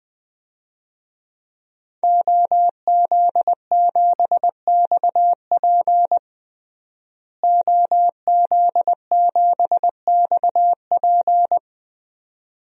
Morse code, heard as a keyed tone, about 20 wpm, 700 Hz. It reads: OZ7XP OZ7XP
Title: DXCC-100 Call Signs - 5 Characters - Repeated 2x - 20wpm